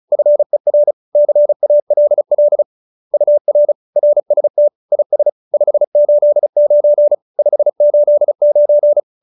Morse code, heard as a keyed tone, about 34 wpm, 590 Hz. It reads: FER CALL UR RST IS 589 589